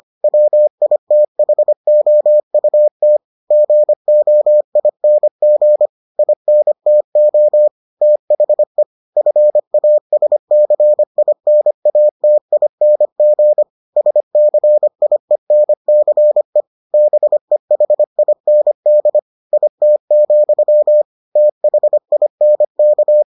Morse code, heard as a keyed tone, about 25 wpm, 600 Hz. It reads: WITHOUT GOING INTO THE FASCINATING SCIENCE BEHIND IT, THINK